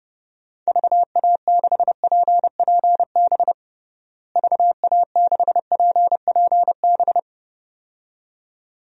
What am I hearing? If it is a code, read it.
Morse code, heard as a keyed tone, about 30 wpm, 700 Hz. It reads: VA6PPB VA6PPB
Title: DXCC-100 Call Signs - 6 Characters - Repeated 2x - 30wpm